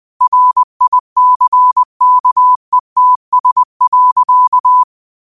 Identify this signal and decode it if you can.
Morse code, heard as a keyed tone, about 20 wpm, 1000 Hz. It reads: RICKETS.
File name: qotdfiles / 20wpm